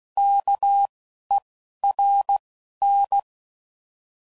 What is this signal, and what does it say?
Morse code, heard as a keyed tone, about 16 wpm, 800 Hz. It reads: K E R N